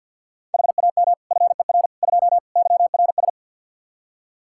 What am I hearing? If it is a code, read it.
Morse code, heard as a keyed tone, about 51 wpm, 700 Hz. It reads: HAM FER 2 YRS
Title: Practice QSO - Exchange 3 - More Info - 50wpm